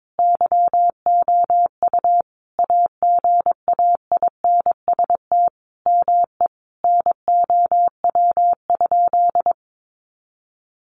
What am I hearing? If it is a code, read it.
Morse code, heard as a keyed tone, about 22 wpm, 700 Hz. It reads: YOU AGAINST ME NOW?
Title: Sentences from Top 400 Words 22wpm